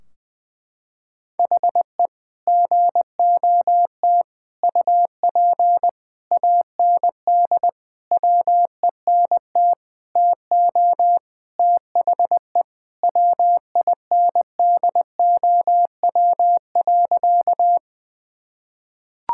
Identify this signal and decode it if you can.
Morse code, heard as a keyed tone, about 20 wpm, 700 Hz. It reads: HE GOT UP AND WENT TO THE WINDOW.
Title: Sentences from Top 1600 Words 20wpm